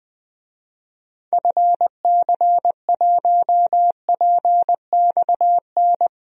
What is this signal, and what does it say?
Morse code, heard as a keyed tone, about 20 wpm, 700 Hz. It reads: FC1PXN
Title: DXCC-100 Call Signs - 6 Characters - 20wpm